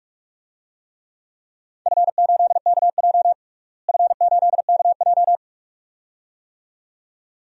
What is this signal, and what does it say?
Morse code, heard as a keyed tone, about 44 wpm, 700 Hz. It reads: F8KJ F8KJ
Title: DXCC-100 Call Signs - 4 Characters - Repeated 2x - 45wpm